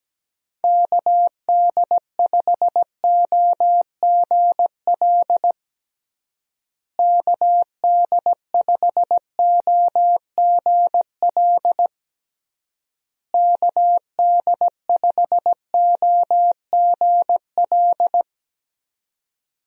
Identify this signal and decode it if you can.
Morse code, heard as a keyed tone, about 17 wpm, 700 Hz. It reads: KD5OGL KD5OGL KD5OGL
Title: USA Call Signs - 6 Characters - Repeated 3x - 17wpm